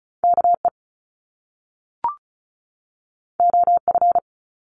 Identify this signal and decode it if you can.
Morse code, heard as a keyed tone, about 36 wpm, 700 Hz. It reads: KE OF
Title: Top Two Letter Combinations in Words 35wpm